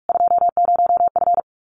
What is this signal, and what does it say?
Morse code, heard as a keyed tone, about 44 wpm, 700 Hz. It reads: 20F
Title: Common QSO Phrases Encore 45wpm